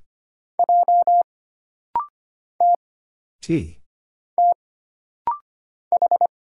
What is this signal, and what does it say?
Morse code, heard as a keyed tone, about 25 wpm, 700 Hz. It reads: J T T H
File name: Single Character (Letter:Number:Punctuation:Prosign) - Rapid-Fire - 25wpm